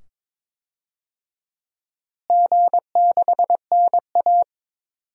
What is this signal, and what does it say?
Morse code, heard as a keyed tone, about 22 wpm, 700 Hz. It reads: G6NA
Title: DXCC-100 Call Signs - 4 Characters - Repeated 2x - 22wpm